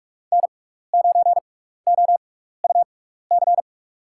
Morse code, heard as a keyed tone, about 46 wpm, spaced as for 15 wpm, 700 Hz. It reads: N9OUC
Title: USA Call Signs - 6x Character Spacing - 45wpm